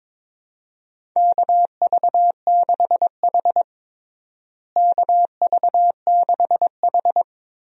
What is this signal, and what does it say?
Morse code, heard as a keyed tone, about 22 wpm, 700 Hz. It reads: KV6H KV6H
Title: USA Call Signs - 4 Characters - Repeated 3x - 22wpm